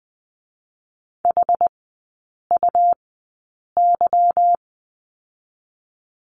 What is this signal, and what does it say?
Morse code, heard as a keyed tone, about 20 wpm, 700 Hz. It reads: H U Y
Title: 178 Lesson 29e - Slash - Three Characters 20wpm